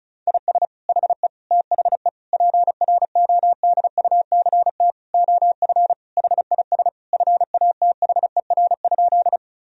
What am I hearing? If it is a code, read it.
Morse code, heard as a keyed tone, about 36 wpm, 700 Hz. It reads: IS HE THE PRODUCT OF HIS FATHER?
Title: Sentences from Top 1000 Words - Review Part 2 - 35wpm